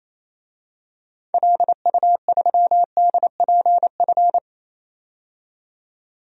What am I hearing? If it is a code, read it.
Morse code, heard as a keyed tone, about 28 wpm, 700 Hz. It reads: LU3DPF
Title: DXCC-100 Call Signs - 6 Characters - 28wpm